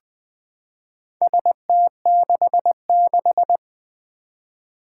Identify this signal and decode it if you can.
Morse code, heard as a keyed tone, about 20 wpm, 700 Hz. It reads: ST66
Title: Four Challenging Letter-Number Combinations 20wpm